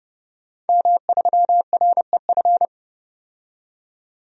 Morse code, heard as a keyed tone, about 30 wpm, 700 Hz. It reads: M3REF
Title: DXCC-100 Call Signs - 5 Characters - 30wpm